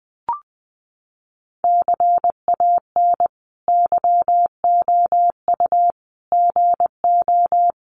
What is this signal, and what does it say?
Morse code, heard as a keyed tone, about 20 wpm, 700 Hz. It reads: CAN YOU GO
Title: Sentences from Top 1500 Words 20wpm